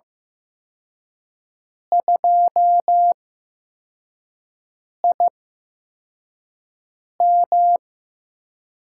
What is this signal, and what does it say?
Morse code, heard as a keyed tone, about 16 wpm, spaced as for 4 wpm, 700 Hz. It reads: 2IM
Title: USA Call Signs - 8x Character Spacing - 15wpm